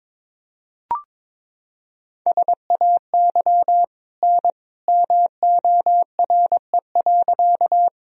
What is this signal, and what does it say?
Morse code, heard as a keyed tone, about 22 wpm, 700 Hz. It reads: SAY N MORE.
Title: Sentences from Top 200 Words and Abbreviations 22wpm